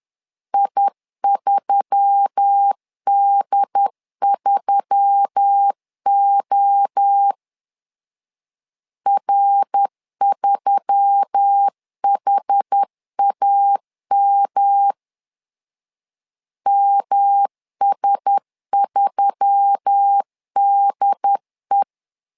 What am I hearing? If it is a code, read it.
Morse code, heard as a keyed tone, about 10 wpm, 800 Hz. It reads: I3D3O R3HAM MS3DE